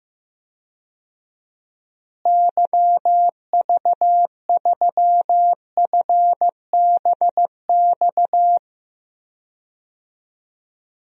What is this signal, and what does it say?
Morse code, heard as a keyed tone, about 15 wpm, 700 Hz. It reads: YV3FBX
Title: DXCC-100 Call Signs - 6 Characters - Repeated 3x - 15wpm